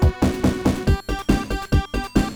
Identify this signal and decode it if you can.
There is no code.